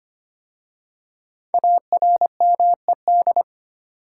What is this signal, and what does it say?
Morse code, heard as a keyed tone, about 25 wpm, 700 Hz. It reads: ARMED
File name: 5 Letter Words - First 3 Letters Given - 25wpm